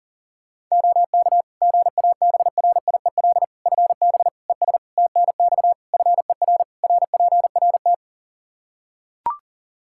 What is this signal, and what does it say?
Morse code, heard as a keyed tone, about 40 wpm, 700 Hz. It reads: OK GABRIEL FB ES TNX FER RPRT
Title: Practice QSO - Exchange 2 - Info - 40wpm